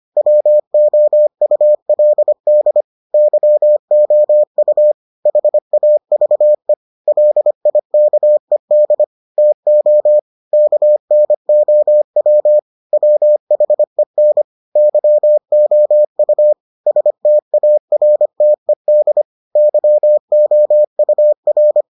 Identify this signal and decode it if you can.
Morse code, heard as a keyed tone, about 25 wpm, 600 Hz. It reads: WOULD YOU HAVE LIKED TO KNOW WHEN YOU STARTED YOUR